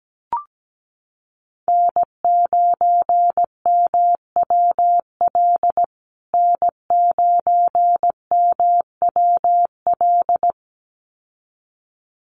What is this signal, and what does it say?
Morse code, heard as a keyed tone, about 17 wpm, 700 Hz. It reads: N9MWL N9MWL
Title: USA Call Signs Repeated 2x 17wpm